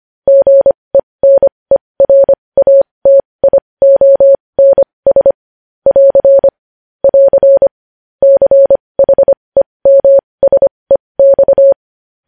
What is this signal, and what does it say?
Morse code, heard as a keyed tone, about 25 wpm, spaced as for 20 wpm, 550 Hz. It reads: GENERATIONS <AR> <AR> CHEMSEX
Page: news-headlines-in-morse-code-at-20-wpm